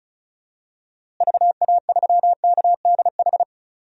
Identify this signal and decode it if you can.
Morse code, heard as a keyed tone, about 36 wpm, 700 Hz. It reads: VA3KDH
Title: DXCC-100 Call Signs - 6 Characters - Repeated 2x - 35wpm